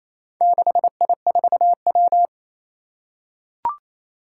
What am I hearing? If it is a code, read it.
Morse code, heard as a keyed tone, about 27 wpm, 700 Hz. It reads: 6I4W
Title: Four Challenging Letter-Number Combinations 28wpm